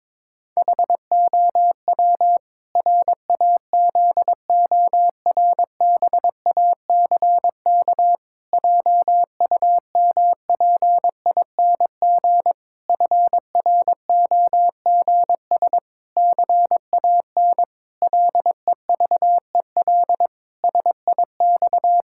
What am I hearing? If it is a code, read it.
Morse code, heard as a keyed tone, about 22 wpm, 700 Hz. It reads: HOW RAZORBACK JUMPING FROGS CAN LEVEL SIX
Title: Pangram Challenge v2 - 22wpm